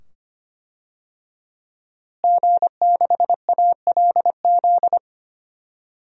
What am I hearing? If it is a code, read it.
Morse code, heard as a keyed tone, about 25 wpm, 700 Hz. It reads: G6ALZ
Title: DXCC-100 Call Signs - 5 Characters - Repeated 2x - 25wpm